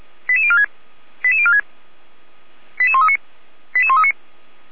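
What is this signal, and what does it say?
Five-tone selective call (ZVEI), heard as quick runs of five tones, tones of about 70 ms. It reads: 88046 88046 88139 88139
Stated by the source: ZVEI five-tone call received on SDR, fire brigade Orlamünde (feuerwehr-orlamuende) server.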